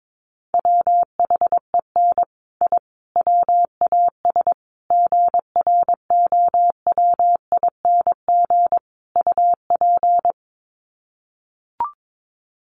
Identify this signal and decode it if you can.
Morse code, heard as a keyed tone, about 22 wpm, 700 Hz. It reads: WHEN I WAS GROWING UP
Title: Sets of 5 Words Encore 22wpm